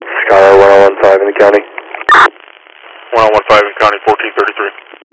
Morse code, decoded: none

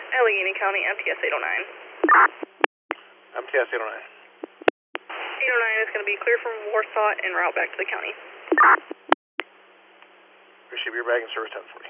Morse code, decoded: none

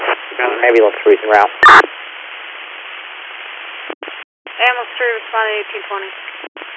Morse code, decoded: none

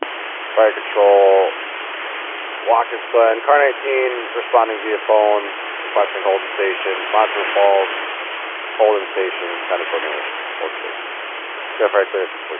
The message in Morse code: none

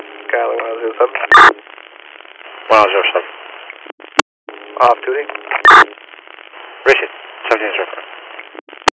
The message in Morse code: none

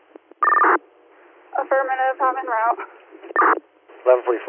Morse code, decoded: none

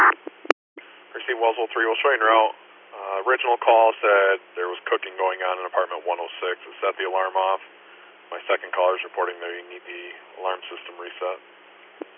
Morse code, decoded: none

